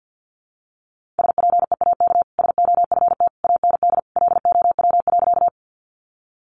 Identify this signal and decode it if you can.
Morse code, heard as a keyed tone, about 50 wpm, 700 Hz. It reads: SPEAK SOFT AND LOW.
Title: Sentences from Top 1500 Words 50wpm